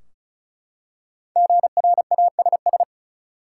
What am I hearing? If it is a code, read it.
Morse code, heard as a keyed tone, about 36 wpm, 700 Hz. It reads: GRASS
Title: Five Letter Words - 35wpm